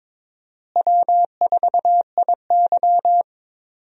Morse code, heard as a keyed tone, about 22 wpm, 700 Hz. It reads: W4IY
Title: USA Call Signs - 4 Characters - Repeated 3x - 22wpm